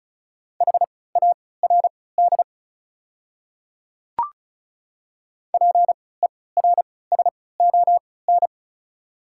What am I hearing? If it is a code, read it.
Morse code, heard as a keyed tone, about 34 wpm, spaced as for 20 wpm, 700 Hz. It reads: HARD PERSON
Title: 3x Character Spacing - Top 500 Words - 35wpm